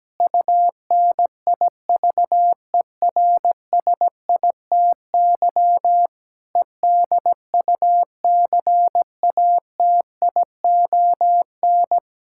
Morse code, heard as a keyed tone, about 17 wpm, 700 Hz. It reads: UNIVERSITY EDUCATION
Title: Sentences from Top 1000 Words - Review Part 6 - 17wpm